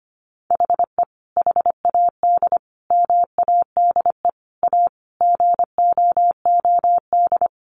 Morse code, heard as a keyed tone, about 25 wpm, 700 Hz. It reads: HE HAD MADE A GOOD
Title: Sentences from Top 900 Words 25wpm